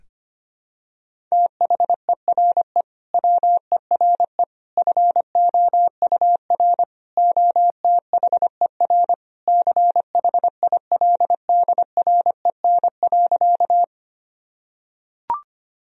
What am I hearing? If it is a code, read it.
Morse code, heard as a keyed tone, about 25 wpm, 700 Hz. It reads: THERE WERE FOUR OTHER CHILDREN.
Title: Sentences from Top 1000 Words - Review Part 1 - 25wpm